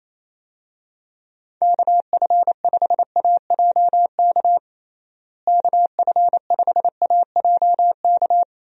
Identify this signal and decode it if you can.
Morse code, heard as a keyed tone, about 27 wpm, 700 Hz. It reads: KF5AJK KF5AJK
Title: USA Call Signs - 6 Characters - Repeated 2x - 28wpm